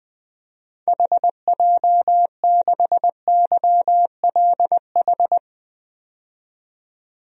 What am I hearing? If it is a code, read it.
Morse code, heard as a keyed tone, about 20 wpm, 700 Hz. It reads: HJ6YLH
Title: DXCC-100 Call Signs - 6 Characters - 20wpm